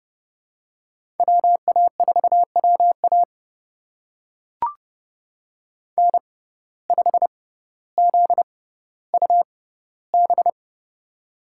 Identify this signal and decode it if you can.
Morse code, heard as a keyed tone, about 30 wpm, spaced as for 15 wpm, 700 Hz. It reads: WA4WA N5ZUB
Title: USA Call Signs - 6x Character Spacing - 30wpm